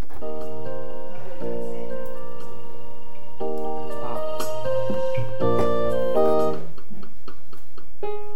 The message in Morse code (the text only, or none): none